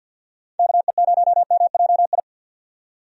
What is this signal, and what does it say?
Morse code, heard as a keyed tone, about 45 wpm, 700 Hz. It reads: KE0MJI
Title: USA Call Signs - 6 Characters - 50wpm